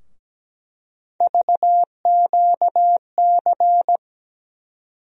Morse code, heard as a keyed tone, about 17 wpm, 700 Hz. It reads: VQC
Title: USA Call Sign Postfixes - 3 Letters 17wpm